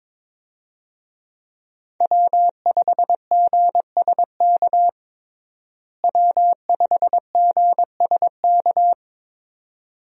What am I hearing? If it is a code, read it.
Morse code, heard as a keyed tone, about 22 wpm, 700 Hz. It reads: W5GSK W5GSK
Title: USA Call Signs - 5 Characters - Repeated 2x - 22wpm